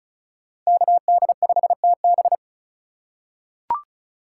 Morse code, heard as a keyed tone, about 36 wpm, 700 Hz. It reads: KD5TB
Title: USA Call Signs - 5 Characters - 2x Character Spacing - 35wpm